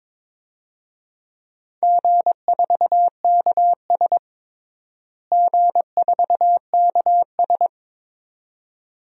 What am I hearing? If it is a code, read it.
Morse code, heard as a keyed tone, about 22 wpm, 700 Hz. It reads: G4KS G4KS
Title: DXCC-100 Call Signs - 4 Characters - Repeated 2x - 22wpm